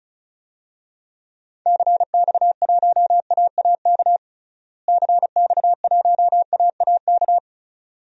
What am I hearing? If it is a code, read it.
Morse code, heard as a keyed tone, about 36 wpm, 700 Hz. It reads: CX1AAK CX1AAK
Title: DXCC-100 Call Signs - 6 Characters - Repeated 2x - 35wpm